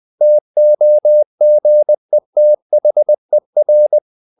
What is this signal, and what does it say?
Morse code, heard as a keyed tone, about 20 wpm, 600 Hz. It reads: TOGETHER
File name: the old man and the sea / the old man and the sea part 1 20wpm